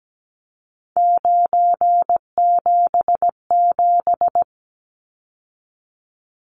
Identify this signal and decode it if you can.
Morse code, heard as a keyed tone, about 17 wpm, 700 Hz. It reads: 977
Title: Numbers - 3-Digit 17wpm